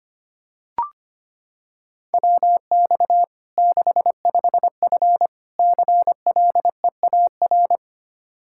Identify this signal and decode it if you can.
Morse code, heard as a keyed tone, about 25 wpm, 700 Hz. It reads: WX 65F CLEAR
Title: Practice QSO - Exchange 2 - Info - 25wpm